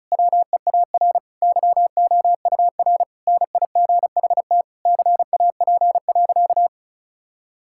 Morse code, heard as a keyed tone, about 36 wpm, 700 Hz. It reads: WEAR YOUR NIGHT CAP.
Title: Sentences from Top 1600 Words 35wpm